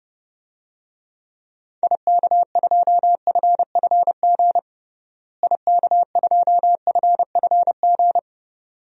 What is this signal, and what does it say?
Morse code, heard as a keyed tone, about 32 wpm, 700 Hz. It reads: IK2FFG IK2FFG
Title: DXCC-100 Call Signs - 6 Characters - Repeated 2x - 30wpm